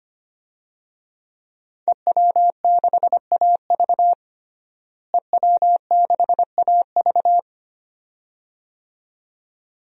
Morse code, heard as a keyed tone, about 25 wpm, 700 Hz. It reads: EW6AV EW6AV
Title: DXCC-100 Call Signs - 5 Characters - Repeated 2x - 25wpm